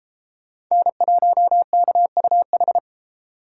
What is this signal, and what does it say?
Morse code, heard as a keyed tone, about 33 wpm, 700 Hz. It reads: N1KUH